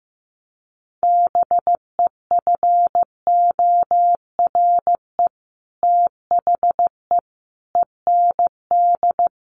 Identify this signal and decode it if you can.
Morse code, heard as a keyed tone, about 15 wpm, 700 Hz. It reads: BEFORE THE END
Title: Sets of 5 Words Encore 15wpm